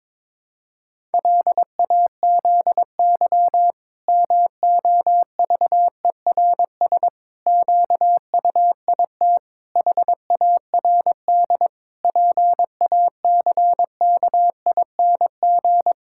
Morse code, heard as a keyed tone, about 22 wpm, 700 Hz. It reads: LAZY MOVERS QUIT HARD PACKING
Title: Pangram Challenge v2 - 22wpm